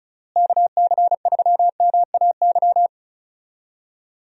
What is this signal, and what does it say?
Morse code, heard as a keyed tone, about 36 wpm, 700 Hz. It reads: KC3MAY
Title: USA Call Signs - 6 Characters - Repeated 3x - 35wpm